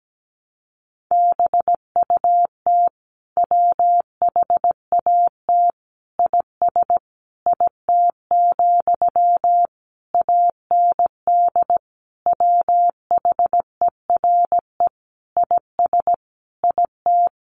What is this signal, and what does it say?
Morse code, heard as a keyed tone, about 17 wpm, 700 Hz. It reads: BUT WHAT IS IT, AND WHERE IS IT
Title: Sentences from Top 300 Words 17wpm